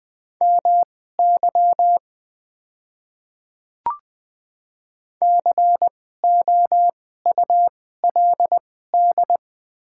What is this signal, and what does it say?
Morse code, heard as a keyed tone, about 20 wpm, spaced as for 14 wpm, 700 Hz. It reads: MY COULD
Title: 2x Character Spacing - Top 100 Words - 20wpm